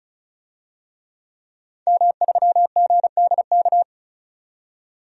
Morse code, heard as a keyed tone, about 36 wpm, 700 Hz. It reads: M3GDK
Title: DXCC-100 Call Signs - 5 Characters - 35wpm